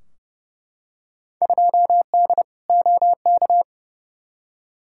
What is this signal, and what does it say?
Morse code, heard as a keyed tone, about 30 wpm, 700 Hz. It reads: 2D OK
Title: Field Day Exchanges Encore - 30wpm